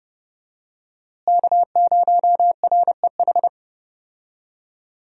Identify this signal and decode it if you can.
Morse code, heard as a keyed tone, about 30 wpm, 700 Hz. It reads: K0REH
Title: USA Call Signs - 5 Characters - Repeated 2x - 30wpm